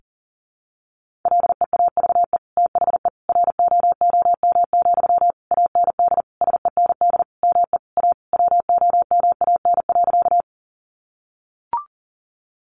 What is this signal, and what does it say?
Morse code, heard as a keyed tone, about 37 wpm, 700 Hz. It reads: LEAVE THE ROOM, AND SEND ME A WOMAN.
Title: Sentences from Top 1500 Words 40wpm